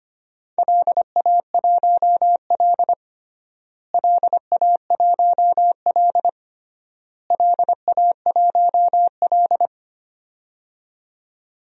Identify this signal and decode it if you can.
Morse code, heard as a keyed tone, about 25 wpm, 700 Hz. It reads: LA1L LA1L LA1L
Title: DXCC-100 Call Signs - 4 Characters - Repeated 3x - 25wpm